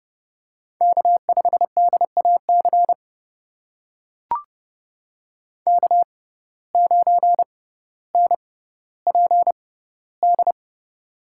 Morse code, heard as a keyed tone, about 30 wpm, spaced as for 15 wpm, 700 Hz. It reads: K5DAC K9NPD
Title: USA Call Signs - 5 Characters - 6x Character Spacing - 30wpm